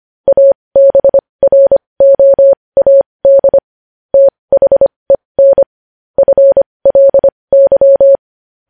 Morse code, heard as a keyed tone, about 25 wpm, spaced as for 20 wpm, 550 Hz. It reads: ABROAD THEN FLY
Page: news-headlines-in-morse-code-at-20-wpm